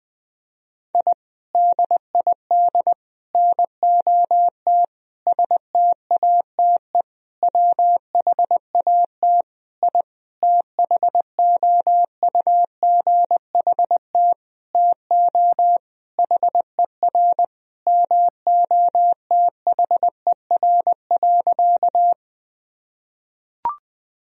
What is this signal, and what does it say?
Morse code, heard as a keyed tone, about 20 wpm, 700 Hz. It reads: I DID NOT STATE WHAT I THOUGHT TO HER MOTHER.
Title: Sentences from Top 1000 Words - Review Part 2 - 20wpm